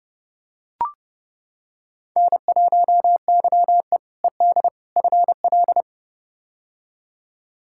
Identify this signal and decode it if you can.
Morse code, heard as a keyed tone, about 32 wpm, 700 Hz. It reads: N1YE ED FL
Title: K1USN SST QSO Practice - 30wpm